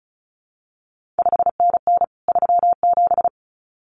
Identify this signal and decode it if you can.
Morse code, heard as a keyed tone, about 36 wpm, 700 Hz. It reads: 5NN 37
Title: CQ WW DX Contest Exchange Encore - 35wpm